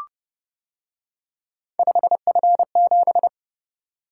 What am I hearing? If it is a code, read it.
Morse code, heard as a keyed tone, about 31 wpm, 700 Hz. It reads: 5F7